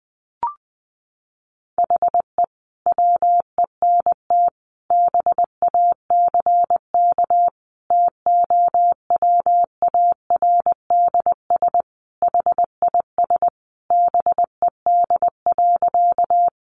Morse code, heard as a keyed tone, about 20 wpm, 700 Hz. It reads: HE WENT BACK TOWARDS HIS BED.
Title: Sentences from Top 1100 Words 20wpm